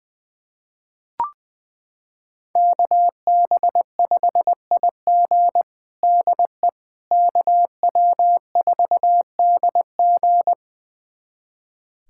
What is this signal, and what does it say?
Morse code, heard as a keyed tone, about 20 wpm, 700 Hz. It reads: KB5IG DE KW4DG
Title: USA Callsign DE Callsign - 5 Characters - 20wpm